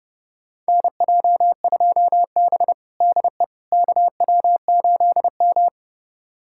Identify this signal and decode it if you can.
Morse code, heard as a keyed tone, about 32 wpm, 700 Hz. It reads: NJ2B DE KW8M
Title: USA Callsign DE Callsign - 4 Characters - 30wpm